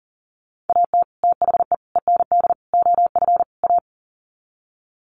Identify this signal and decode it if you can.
Morse code, heard as a keyed tone, about 40 wpm, 700 Hz. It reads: AT THE END OF A
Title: Sets of 5 Words 40wpm